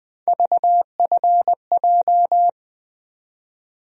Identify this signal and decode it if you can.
Morse code, heard as a keyed tone, about 20 wpm, 700 Hz. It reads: VFJ
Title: USA Call Sign Postfixes - 3 Letters 20wpm